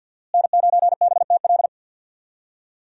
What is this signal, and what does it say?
Morse code, heard as a keyed tone, about 52 wpm, 700 Hz. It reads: N9BTL
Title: USA Call Signs - 5 Characters - Repeated 3x - 50wpm